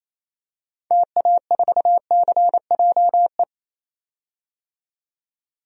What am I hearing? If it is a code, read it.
Morse code, heard as a keyed tone, about 27 wpm, 700 Hz. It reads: TA4CJE
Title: DXCC-100 Call Signs - 6 Characters - 28wpm